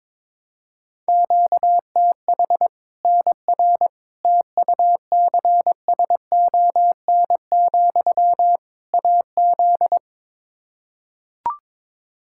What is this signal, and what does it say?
Morse code, heard as a keyed tone, about 22 wpm, 700 Hz. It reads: QTH NR TUCSON, AZ
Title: K1USN SST QSO Practice 22wpm